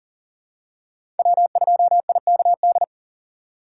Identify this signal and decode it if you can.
Morse code, heard as a keyed tone, about 40 wpm, 700 Hz. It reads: W2IKD
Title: USA Call Signs - 5 Characters - 3x Character Spacing - 40wpm